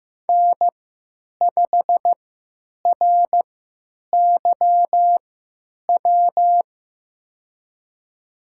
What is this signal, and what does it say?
Morse code, heard as a keyed tone, about 15 wpm, spaced as for 9 wpm, 700 Hz. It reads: N5RYW
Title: USA Call Signs - 3x Character Spacing - 15wpm